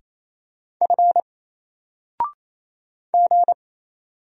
Morse code, heard as a keyed tone, about 27 wpm, 700 Hz. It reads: F G
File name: Single-Letter-Number-28wpm